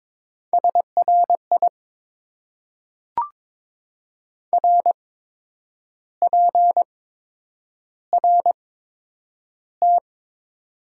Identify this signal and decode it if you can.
Morse code, heard as a keyed tone, about 21 wpm, spaced as for 8 wpm, 700 Hz. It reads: SRI RPRT
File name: Top 90 QSO Elements - 8x Character Spacing - 22wpm